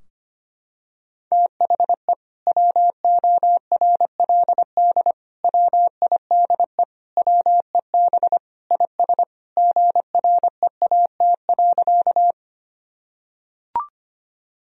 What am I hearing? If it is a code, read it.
Morse code, heard as a keyed tone, about 25 wpm, 700 Hz. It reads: THE WORLD WIDE WEB IS GREAT.
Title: Sentences from Top 1000 Words - Review Part 4 - 25wpm